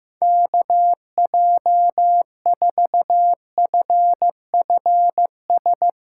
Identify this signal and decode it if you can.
Morse code, heard as a keyed tone, about 15 wpm, 700 Hz. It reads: KJ4FFS
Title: USA Callsign DE Callsign - 6 Characters - 15wpm